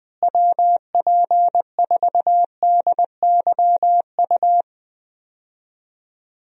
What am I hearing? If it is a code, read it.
Morse code, heard as a keyed tone, about 20 wpm, 700 Hz. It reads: WP4DYU
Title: USA Call Signs - 6 Characters - Repeated 2x - 20wpm